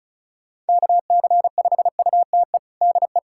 Morse code, heard as a keyed tone, about 36 wpm, 700 Hz. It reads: KC5UTE DE
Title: USA Callsign DE Callsign 35wpm